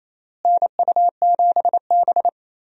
Morse code, heard as a keyed tone, about 28 wpm, 700 Hz. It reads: NU7B
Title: USA Call Signs - 4 Characters - Repeated 3x - 28wpm